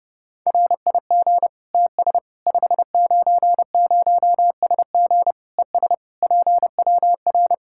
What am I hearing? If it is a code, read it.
Morse code, heard as a keyed tone, about 30 wpm, 700 Hz. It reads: RIG TS 590SG ES PWR
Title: Practice QSO - Exchange 2 - Info - 30wpm